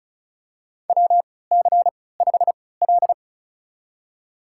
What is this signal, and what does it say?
Morse code, heard as a keyed tone, about 36 wpm, spaced as for 20 wpm, 700 Hz. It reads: WC5L
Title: USA Call Signs - 4 Characters - 3x Character Spacing - 35wpm